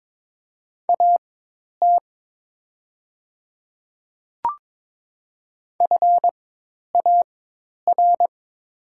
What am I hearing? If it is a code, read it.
Morse code, heard as a keyed tone, about 22 wpm, spaced as for 10 wpm, 700 Hz. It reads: AT FAR
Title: 4x Character Spacing - Top 500 Words - 22wpm